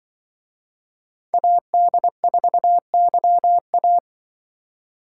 Morse code, heard as a keyed tone, about 24 wpm, 700 Hz. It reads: AD4YA
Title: Speed Racing - USA 2x2 Call Signs 20wpm